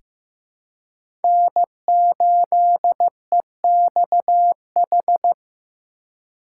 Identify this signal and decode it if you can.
Morse code, heard as a keyed tone, about 15 wpm, 700 Hz. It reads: N8EXH
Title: USA Call Signs - 4x Character Spacing - 15wpm